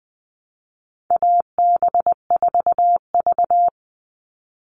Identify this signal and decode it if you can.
Morse code, heard as a keyed tone, about 20 wpm, 700 Hz. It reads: AB4V
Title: All Cut Numbers - 4-Digit 20wpm